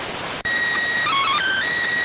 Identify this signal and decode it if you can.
SSTV (slow-scan television), PD 290.